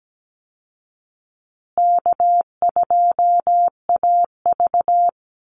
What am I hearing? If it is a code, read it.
Morse code, heard as a keyed tone, about 17 wpm, 700 Hz. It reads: K2AV
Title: Easy Call Signs Repeated 2x - 17wpm